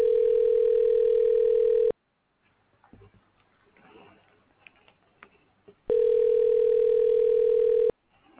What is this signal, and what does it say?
Ringback tone (North America), 440 + 480 Hz.